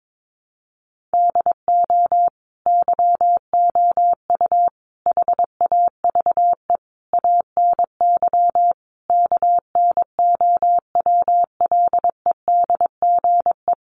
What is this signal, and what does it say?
Morse code, heard as a keyed tone, about 22 wpm, 700 Hz. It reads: DO YOU HAVE ANY KNOWLEDGE
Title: Sentences from Top 600 Words 22wpm